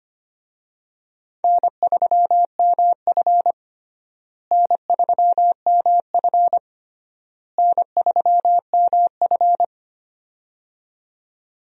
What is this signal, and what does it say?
Morse code, heard as a keyed tone, about 25 wpm, 700 Hz. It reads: N3MF N3MF N3MF
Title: USA Call Signs - 4 Characters - Repeated 3x - 25wpm